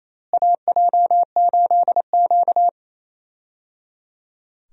Morse code, heard as a keyed tone, about 28 wpm, 700 Hz. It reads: AJ8Q
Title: USA Call Signs - 4 Characters - Repeated 3x - 28wpm